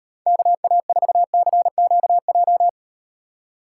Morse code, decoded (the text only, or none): KA4CQJ